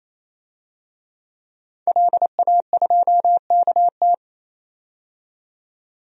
Morse code, heard as a keyed tone, about 28 wpm, 700 Hz. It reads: LA2KT
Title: DXCC-100 Call Signs - 5 Characters - 28wpm